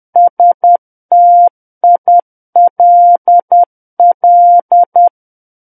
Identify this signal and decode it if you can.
Morse code, heard as a keyed tone, about 10 wpm, 700 Hz. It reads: STILL
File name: Book One / 01-Chapter 1 - 10 WPM